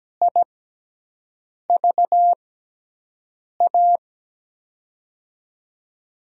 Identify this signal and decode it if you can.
Morse code, heard as a keyed tone, about 17 wpm, spaced as for 6 wpm, 700 Hz. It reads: IVA
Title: USA Call Signs - 6 Characters - 6x Character Spacing - 17wpm